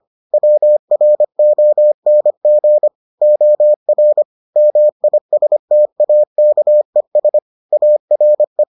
Morse code, heard as a keyed tone, about 25 wpm, 600 Hz. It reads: WRONG OR MISTAKES ARE